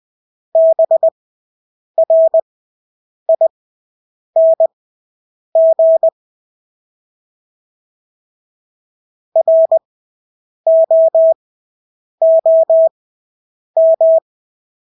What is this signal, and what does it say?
Morse code, heard as a keyed tone, about 20 wpm, spaced as for 8 wpm, 650 Hz. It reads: BRING ROOM
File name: Engelske ord 20/8 wpm